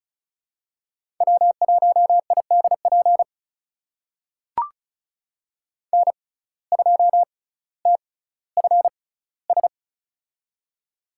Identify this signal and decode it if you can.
Morse code, heard as a keyed tone, about 36 wpm, spaced as for 18 wpm, 700 Hz. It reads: W1IDP N2TFS
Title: USA Call Signs - 5 Characters - 6x Character Spacing - 35wpm